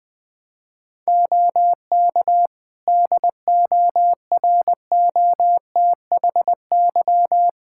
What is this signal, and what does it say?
Morse code, heard as a keyed tone, about 21 wpm, 700 Hz. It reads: OK DOROTHY
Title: Practice QSO - Exchange 4 - Ending - 20wpm